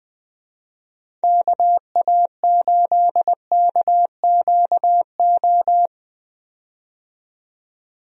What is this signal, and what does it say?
Morse code, heard as a keyed tone, about 20 wpm, 700 Hz. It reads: KA8KQO
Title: USA Call Signs - 6 Characters - 20wpm